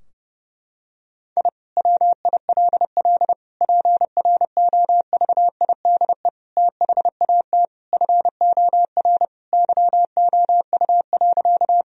Morse code, heard as a keyed tone, about 30 wpm, 700 Hz. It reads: I WILL PROVIDE THAT FOR YOU.
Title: Sentences from Top 1000 Words - Review Part 3 - 30wpm